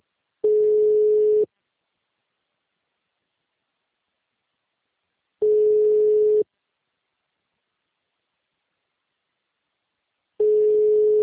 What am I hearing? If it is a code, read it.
Ringback tone (Europe, 425 Hz).
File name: out- -2003- - - .120